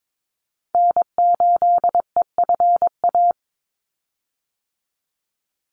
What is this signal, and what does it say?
Morse code, heard as a keyed tone, about 22 wpm, 700 Hz. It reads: N8EFA
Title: USA Call Signs 22wpm